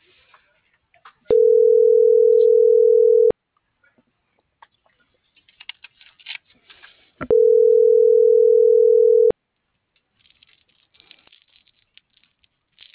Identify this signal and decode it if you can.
Ringback tone (North America), 440 + 480 Hz.
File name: CENTRAL MULTIMODAL / out alfredovelazquez